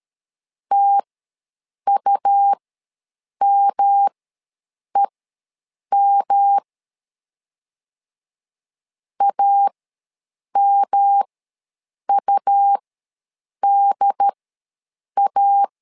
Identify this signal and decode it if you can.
Morse code, heard as a keyed tone, about 13 wpm, spaced as for 7 wpm, 800 Hz. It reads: TUMEM AMUDA
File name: LEZIONI / 3 MWRO / 3 M ws 12wpm 1745